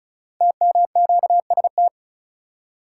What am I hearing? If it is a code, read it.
Morse code, heard as a keyed tone, about 36 wpm, 700 Hz. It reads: TMQST
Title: Code Groups - 5 Letter:Numbers - 35wpm